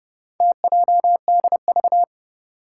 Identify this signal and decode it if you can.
Morse code, heard as a keyed tone, about 30 wpm, 700 Hz. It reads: TJDV
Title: Four Challenging Letter-Number Combinations 30wpm